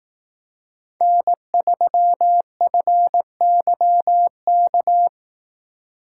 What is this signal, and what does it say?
Morse code, heard as a keyed tone, about 18 wpm, 700 Hz. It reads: N3FYK
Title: Speed Racing - USA 1x3 Call Signs 15wpm